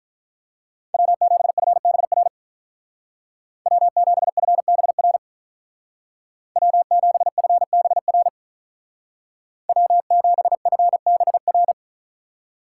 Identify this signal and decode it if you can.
Morse code, heard as a keyed tone, about 47 wpm, 700 Hz. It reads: W7FBR W7FBR W7FBR W7FBR